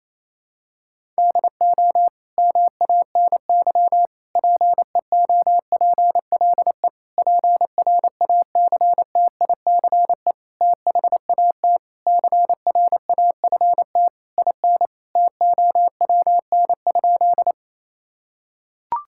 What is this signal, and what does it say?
Morse code, heard as a keyed tone, about 28 wpm, 700 Hz. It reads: DO MANY PEOPLE PRACTICE THAT CRAFT IN TOWN?
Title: Sentences from Top 1000 Words - Review Part 6 - 28wpm